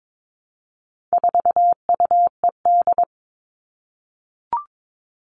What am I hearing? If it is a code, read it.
Morse code, heard as a keyed tone, about 22 wpm, 700 Hz. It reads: 4UED
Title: All Cut Numbers - 4-Digit 22wpm